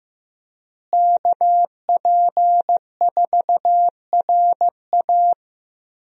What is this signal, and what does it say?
Morse code, heard as a keyed tone, about 15 wpm, 700 Hz. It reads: KP4RA
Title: DXCC-100 Call Signs - 5 Characters - Repeated 3x - 15wpm